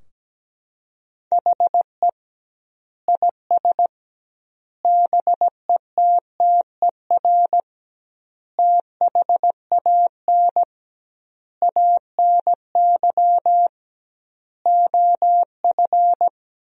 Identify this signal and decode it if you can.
Morse code, heard as a keyed tone, about 17 wpm, 700 Hz. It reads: HE IS BETTER THAN ANY OF
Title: 2x Word-Spacing - Sentences from Top 400 Words - 17wpm